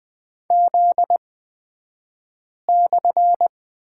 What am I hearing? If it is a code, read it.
Morse code, heard as a keyed tone, about 20 wpm, spaced as for 5 wpm, 700 Hz. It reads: Z/
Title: 242 Lesson 39d - Letter z - Two Characters 20wpm